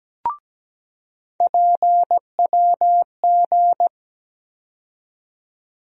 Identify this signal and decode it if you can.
Morse code, heard as a keyed tone, about 17 wpm, 700 Hz. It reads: PWG
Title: USA Call Sign Postfixes - 3 Letters 17wpm